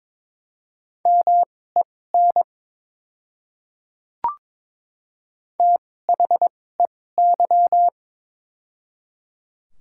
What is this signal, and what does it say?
Morse code, heard as a keyed tone, about 22 wpm, spaced as for 16 wpm, 700 Hz. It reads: MEN THEY
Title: 2x Character Spacing - Top 500 Words - 22wpm